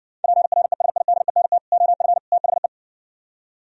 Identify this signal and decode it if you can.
Morse code, heard as a keyed tone, about 68 wpm, 700 Hz. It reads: PRESIDENT OF THE